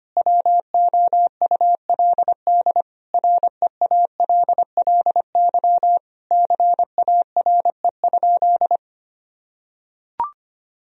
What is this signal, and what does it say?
Morse code, heard as a keyed tone, about 25 wpm, 700 Hz. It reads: WOULD REALLY CARE?
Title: Sentences from Top 1000 Words - Review Part 3 - 25wpm